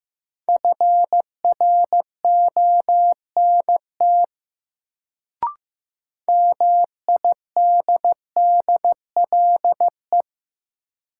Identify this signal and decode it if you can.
Morse code, heard as a keyed tone, about 15 wpm, 700 Hz. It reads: FRONT MIDDLE
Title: Sets of 2 Words - New Word Drill - 15wpm